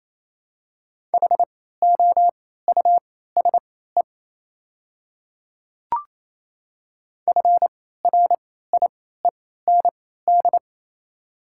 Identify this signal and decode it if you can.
Morse code, heard as a keyed tone, about 27 wpm, spaced as for 16 wpm, 690 Hz. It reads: HOUSE FRIEND
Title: 3x Character Spacing - Top 300 Words - 28wpm